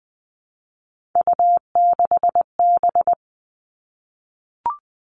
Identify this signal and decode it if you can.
Morse code, heard as a keyed tone, about 20 wpm, 700 Hz. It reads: U6B
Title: All Cut Numbers - 3-Digit 20wpm